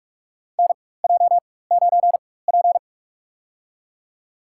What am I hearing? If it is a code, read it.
Morse code, heard as a keyed tone, about 41 wpm, spaced as for 20 wpm, 690 Hz. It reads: NJ9P
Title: USA Call Signs - 4x Character Spacing - 45wpm